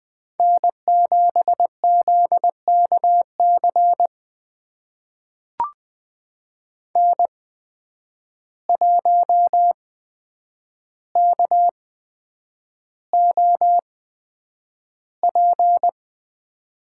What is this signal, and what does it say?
Morse code, heard as a keyed tone, about 20 wpm, spaced as for 9 wpm, 700 Hz. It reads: N7ZKC N1KOP
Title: USA Call Signs - 5 Characters - 8x Character Spacing - 20wpm